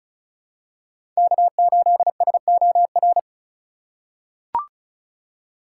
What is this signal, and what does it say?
Morse code, heard as a keyed tone, about 34 wpm, 700 Hz. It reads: K8SOR
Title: USA Call Signs - 5 Characters - 2x Character Spacing - 35wpm